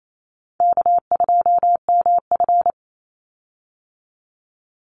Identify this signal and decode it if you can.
Morse code, heard as a keyed tone, about 28 wpm, 700 Hz. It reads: K2MF
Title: Easy Call Signs Repeated 3x - 28wpm